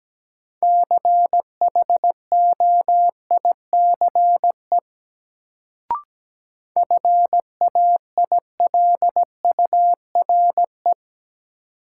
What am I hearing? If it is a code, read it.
Morse code, heard as a keyed tone, about 17 wpm, 700 Hz. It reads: CHOICE FAILURE
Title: Sentences from Top 900 Words - New Word Drill - 17wpm